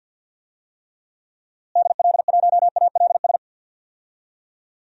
Morse code, heard as a keyed tone, about 52 wpm, 700 Hz. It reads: DL1ALS
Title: DXCC-100 Call Signs - 6 Characters - Repeated 3x - 50wpm